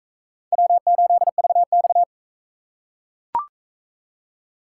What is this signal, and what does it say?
Morse code, heard as a keyed tone, about 40 wpm, 700 Hz. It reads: W8VX